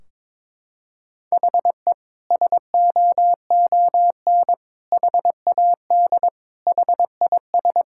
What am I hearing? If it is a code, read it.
Morse code, heard as a keyed tone, about 22 wpm, 700 Hz. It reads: HE SOON HAD HIS
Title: Sentences from Top 1600 Words 22wpm